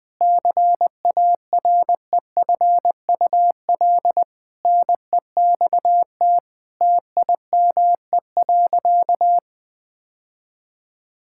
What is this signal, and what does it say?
Morse code, heard as a keyed tone, about 20 wpm, 700 Hz. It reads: CAREFUL NEXT TIME.
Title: Sentences from Top 1600 Words 20wpm